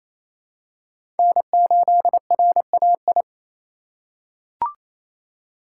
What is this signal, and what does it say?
Morse code, heard as a keyed tone, about 27 wpm, 700 Hz. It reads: N8RAI
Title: USA Call Signs - 5 Characters - 2x Character Spacing - 28wpm